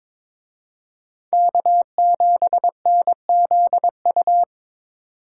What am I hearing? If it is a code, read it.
Morse code, heard as a keyed tone, about 26 wpm, 700 Hz. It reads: K7NZU
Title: Speed Racing - USA 1x3 Call Signs 22wpm